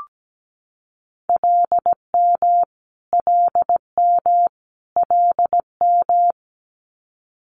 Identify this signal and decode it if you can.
Morse code, heard as a keyed tone, about 17 wpm, 700 Hz. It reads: LM LM LM
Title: Multi-letter Phonemes Repeated 3x 17wpm